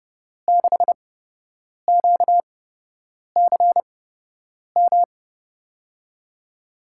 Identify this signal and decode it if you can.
Morse code, heard as a keyed tone, about 30 wpm, spaced as for 8 wpm, 700 Hz. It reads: 6QCM
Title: USA Call Signs - 5 Characters - 8x Character Spacing - 30wpm